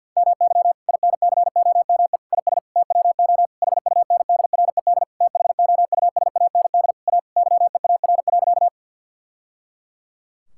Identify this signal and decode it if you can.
Morse code, heard as a keyed tone, about 48 wpm, 690 Hz. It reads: MY INCOME IS TWO HUNDRED THOUSAND A YEAR.
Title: Sentences from Top 1000 Words - Review Part 5 - 50wpm